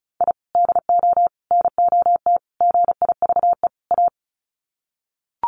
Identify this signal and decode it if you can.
Morse code, heard as a keyed tone, about 34 wpm, 700 Hz. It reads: I DO NOT GIVE A
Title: Sets of 5 Words Encore2 35wpm